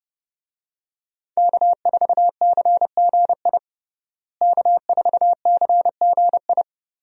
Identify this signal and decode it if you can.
Morse code, heard as a keyed tone, about 30 wpm, 700 Hz. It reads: K4CGI K4CGI
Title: USA Call Signs - 5 Characters - Repeated 3x - 30wpm